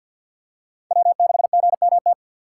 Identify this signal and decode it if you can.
Morse code, heard as a keyed tone, about 49 wpm, 700 Hz. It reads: W6GMT
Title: USA Call Signs - 6x Character Spacing - 50wpm